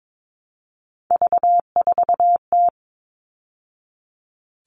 Morse code, heard as a keyed tone, about 22 wpm, 700 Hz. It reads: V4T
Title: All Cut Numbers - 3-Digit 22wpm